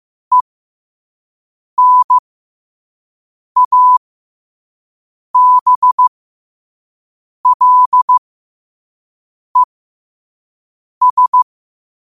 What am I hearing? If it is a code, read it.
Morse code, heard as a keyed tone, about 15 wpm, spaced as for 5 wpm, 1000 Hz. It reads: ENABLES
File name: qotdfiles / 5wpm